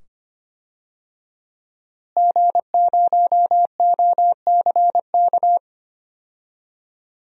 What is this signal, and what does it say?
Morse code, heard as a keyed tone, about 25 wpm, 700 Hz. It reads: G0OCK
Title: DXCC-100 Call Signs - 5 Characters - Repeated 2x - 25wpm